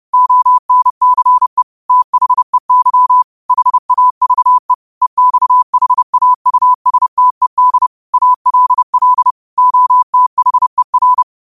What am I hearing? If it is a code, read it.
Morse code, heard as a keyed tone, about 30 wpm, 1000 Hz. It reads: ONCE THEY HAVE EXHAUSTED ALL OTHER
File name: qotdfiles / 30wpm